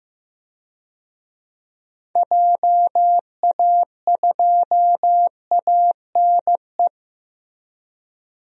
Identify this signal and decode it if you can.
Morse code, heard as a keyed tone, about 15 wpm, 700 Hz. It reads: JA2ANE
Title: DXCC-100 Call Signs - 6 Characters - Repeated 3x - 15wpm